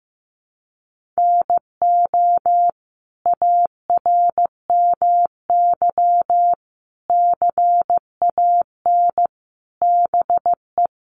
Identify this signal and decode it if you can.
Morse code, heard as a keyed tone, about 15 wpm, 700 Hz. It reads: NO ARMY CAN BE
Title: Sentences from Top 1300 Words 15wpm